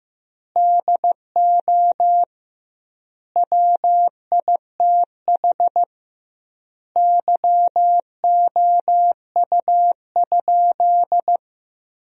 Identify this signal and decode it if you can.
Morse code, heard as a keyed tone, about 15 wpm, 700 Hz. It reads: DO WITH YOU?
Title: 2x Word-Spacing - Sentences from Top 100 Words - 15wpm